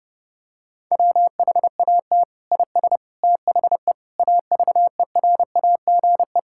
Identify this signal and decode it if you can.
Morse code, heard as a keyed tone, about 32 wpm, 700 Hz. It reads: WHAT IS THE AVERAGE
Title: Sentences from Top 1000 Words - Review Part 6 - 30wpm